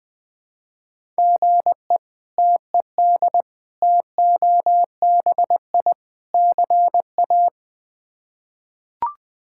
Morse code, heard as a keyed tone, about 20 wpm, 700 Hz. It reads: GE TED TOBI CA
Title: K1USN SST QSO Practice - 20wpm